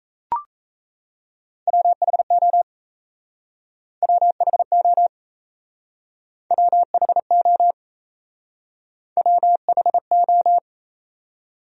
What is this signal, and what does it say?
Morse code, heard as a keyed tone, about 36 wpm, 700 Hz. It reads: WHO WHO WHO WHO